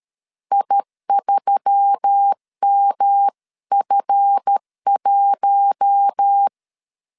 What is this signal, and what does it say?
Morse code, heard as a keyed tone, about 13 wpm, 800 Hz. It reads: I3MF1
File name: CW / DET / 1 9 1500C S C 12wpm 3811